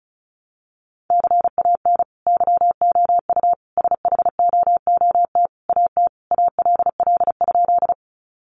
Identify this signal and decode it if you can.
Morse code, heard as a keyed tone, about 34 wpm, 700 Hz. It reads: CAN YOU SHOOT AT ALL?
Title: Sentences from Top 1200 Words 35wpm